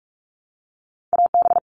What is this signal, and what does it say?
Morse code, heard as a keyed tone, about 44 wpm, 700 Hz. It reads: AB
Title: All Cut Numbers - 2-Digit 45wpm